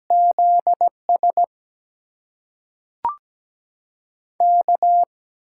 Morse code, heard as a keyed tone, about 17 wpm, 700 Hz. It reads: ZS K
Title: USA Call Signs - 4 Characters - 8x Character Spacing - 17wpm